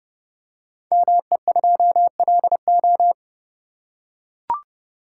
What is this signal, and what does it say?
Morse code, heard as a keyed tone, about 30 wpm, 700 Hz. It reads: ME2LO
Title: Code Groups - 5 Letter:Numbers - 30wpm